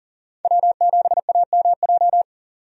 Morse code, heard as a keyed tone, about 37 wpm, 700 Hz. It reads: W7AMJ
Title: USA Call Signs - 5 Characters - 40wpm